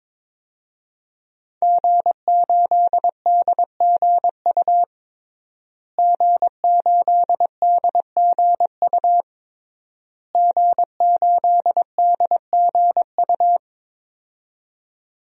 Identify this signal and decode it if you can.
Morse code, heard as a keyed tone, about 22 wpm, 700 Hz. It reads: G8DGU G8DGU G8DGU
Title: DXCC-100 Call Signs - 5 Characters - Repeated 3x - 22wpm